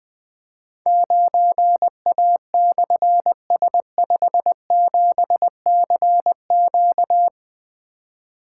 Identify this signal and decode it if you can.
Morse code, heard as a keyed tone, about 20 wpm, 700 Hz. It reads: 9A/S57CQ
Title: Difficult Call Signs 20wpm